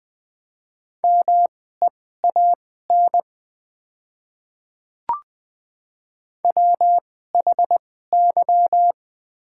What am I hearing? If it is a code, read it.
Morse code, heard as a keyed tone, about 20 wpm, spaced as for 15 wpm, 700 Hz. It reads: MEAN WHY
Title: 2x Character Spacing - Top 300 Words - 20wpm